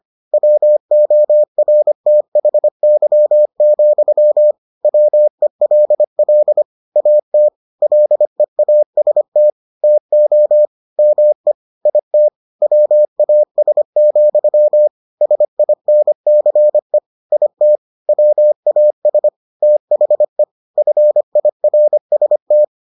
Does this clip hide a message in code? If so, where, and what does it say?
Yes: Morse code from 0.3 to 22.6 s, heard as a keyed tone, about 25 wpm, 600 Hz. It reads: WORTHY, WELL AT LEAST TO ME IT WAS, SINCE IT WAS THE FIRST